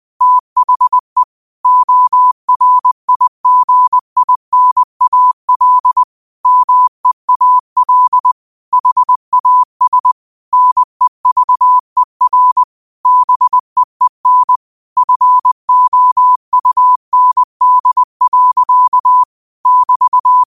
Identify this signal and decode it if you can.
Morse code, heard as a keyed tone, about 20 wpm, 1000 Hz. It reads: THE ORIGINAL MEAL HAS NEVER BEEN FOUND. <BT>
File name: qotdfiles / 20wpm